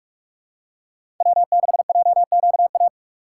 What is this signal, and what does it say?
Morse code, heard as a keyed tone, about 46 wpm, 700 Hz. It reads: W6JQA
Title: USA Call Signs - 5 Characters - 2x Character Spacing - 45wpm